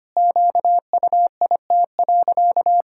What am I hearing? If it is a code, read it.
Morse code, heard as a keyed tone, about 25 wpm, 700 Hz. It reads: QUIT.
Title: Pangram Challenge v2 - 25wpm